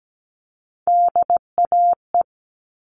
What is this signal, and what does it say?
Morse code, heard as a keyed tone, about 17 wpm, 700 Hz. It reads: DAE
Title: All Cut Numbers - 3-Digit 17wpm